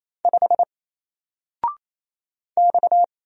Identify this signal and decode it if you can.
Morse code, heard as a keyed tone, about 27 wpm, 700 Hz. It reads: 5 X
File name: Single-Letter-Number-28wpm